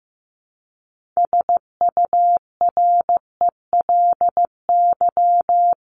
Morse code, heard as a keyed tone, about 15 wpm, 700 Hz. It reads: SURELY
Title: LY Suffix - Top 100 Words with Base Words 15wpm